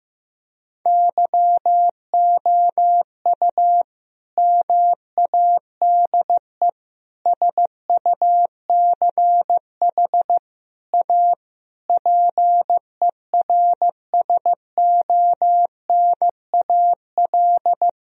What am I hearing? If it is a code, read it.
Morse code, heard as a keyed tone, about 15 wpm, 700 Hz. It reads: YOU MADE SUCH A PERSONAL
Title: Sentences from Top 1000 Words - Review Part 4 - 15wpm